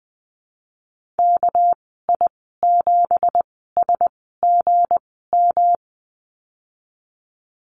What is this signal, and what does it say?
Morse code, heard as a keyed tone, about 20 wpm, spaced as for 15 wpm, 700 Hz. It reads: KI7SGM
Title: 207 Lesson 33g - Letter j - Call Signs 20wpm